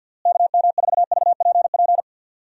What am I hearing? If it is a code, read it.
Morse code, heard as a keyed tone, about 50 wpm, 700 Hz. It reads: KM4VPP
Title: USA Call Signs - 6 Characters - Repeated 3x - 50wpm